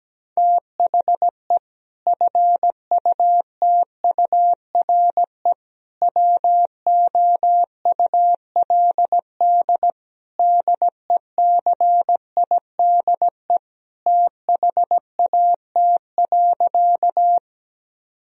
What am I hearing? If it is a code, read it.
Morse code, heard as a keyed tone, about 17 wpm, 700 Hz. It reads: THE FUTURE WOULD DECIDE THAT.
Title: Sentences from Top 1000 Words - Review Part 3 - 17wpm